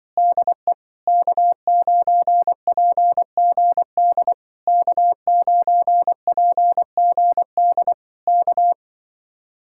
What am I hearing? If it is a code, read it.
Morse code, heard as a keyed tone, about 24 wpm, 700 Hz. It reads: DE K9PGD K9PGD K
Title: Speed-Racing Common QSO Phrases - 20wpm